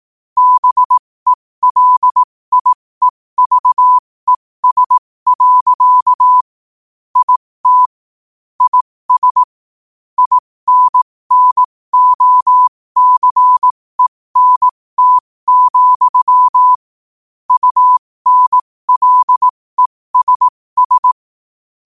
Morse code, heard as a keyed tone, about 18 wpm, spaced as for 15 wpm, 1000 Hz. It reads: BELIEVES. IT IS INNOCENT, UNLESS
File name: 15wpm